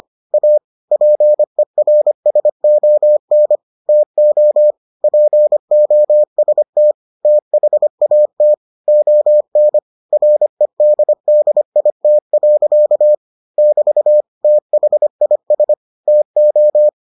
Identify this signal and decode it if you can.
Morse code, heard as a keyed tone, about 25 wpm, 600 Hz. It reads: A PERSON TO POST THAT ON REDDIT. <BT> THIS TO